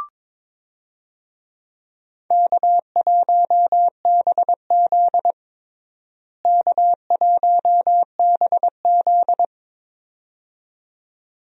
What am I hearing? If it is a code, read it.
Morse code, heard as a keyed tone, about 22 wpm, 700 Hz. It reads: K1BZ K1BZ
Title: USA Call Signs - 4 Characters - Repeated 2x - 22wpm